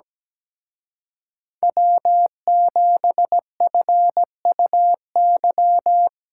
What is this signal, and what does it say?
Morse code, heard as a keyed tone, about 20 wpm, 700 Hz. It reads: W7FUY
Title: Speed Racing - USA 1x3 Call Signs 17wpm